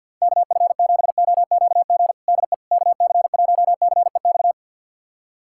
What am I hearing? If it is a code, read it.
Morse code, heard as a keyed tone, about 48 wpm, 700 Hz. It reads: KF7OQG DE KC1CEX
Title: USA Callsign DE Callsign - 6 Characters - 50wpm